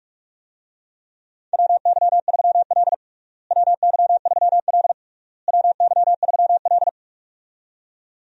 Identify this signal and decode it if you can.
Morse code, heard as a keyed tone, about 46 wpm, 700 Hz. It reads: WY3L WY3L WY3L
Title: USA Call Signs - 4 Characters - Repeated 3x - 45wpm